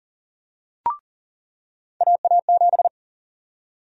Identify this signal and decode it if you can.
Morse code, heard as a keyed tone, about 40 wpm, 700 Hz. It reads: AA7
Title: USA Call Sign Prefixes - 40wpm